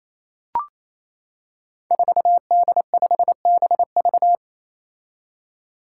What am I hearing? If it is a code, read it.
Morse code, heard as a keyed tone, about 27 wpm, 700 Hz. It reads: 4D5BV
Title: Five Challenging Letter-Number Combinations 28wpm